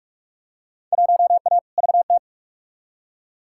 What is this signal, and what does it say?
Morse code, heard as a keyed tone, about 43 wpm, 700 Hz. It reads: 1A VT
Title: Field Day Exchanges Encore - 45wpm